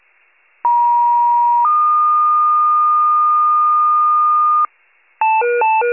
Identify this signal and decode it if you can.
Two-tone page, heard as a one-second tone, then a three-second tone, about 950 then 1220 Hz, with noise. It is heard from 0.6 to 4.7 s.